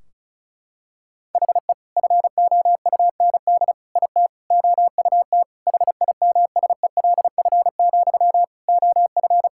Morse code, heard as a keyed tone, about 34 wpm, 700 Hz. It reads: HE FOUND IT OUT HIMSELF, OF
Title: Sentences from Top 1600 Words 35wpm